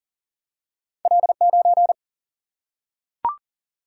Morse code, heard as a keyed tone, about 38 wpm, 700 Hz. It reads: L9
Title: Single-Letter Single-Number 40wpm